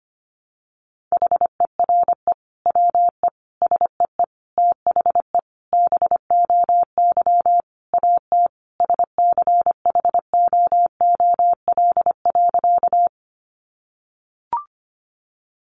Morse code, heard as a keyed tone, about 25 wpm, 700 Hz. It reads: HERE WE SEE THE BOY AT SCHOOL.
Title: Sentences from Top 300 Words 25wpm